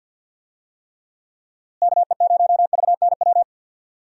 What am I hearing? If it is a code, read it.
Morse code, heard as a keyed tone, about 50 wpm, 700 Hz. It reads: KE0VNW
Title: USA Call Signs - 6 Characters - Repeated 3x - 50wpm